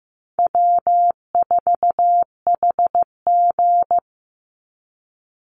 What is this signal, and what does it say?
Morse code, heard as a keyed tone, about 15 wpm, 700 Hz. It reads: W4HG
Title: Easy Call Signs Repeated 2x - 15wpm